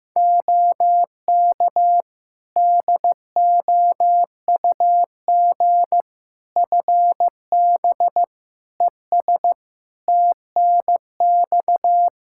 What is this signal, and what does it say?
Morse code, heard as a keyed tone, about 16 wpm, 700 Hz. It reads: OK DOUG FB ES TNX
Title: Practice QSO - Exchange 2 - Info - 15wpm